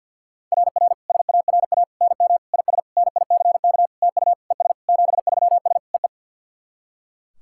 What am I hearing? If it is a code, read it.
Morse code, heard as a keyed tone, about 45 wpm, 690 Hz. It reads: RR SARA NM IS NICK TU ES 73S EE
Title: SOTA QSO Practice 50wpm